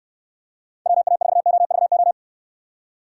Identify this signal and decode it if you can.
Morse code, heard as a keyed tone, about 69 wpm, 700 Hz. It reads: WA4OVO